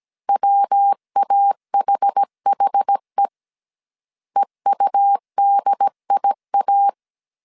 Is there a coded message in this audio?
Yes: Morse code from 0.3 to 6.9 s, heard as a keyed tone, about 17 wpm, 800 Hz. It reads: WAHHE EUDIA